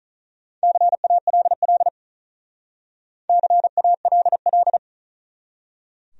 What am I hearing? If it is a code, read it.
Morse code, heard as a keyed tone, about 44 wpm, 700 Hz. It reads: CALL CALL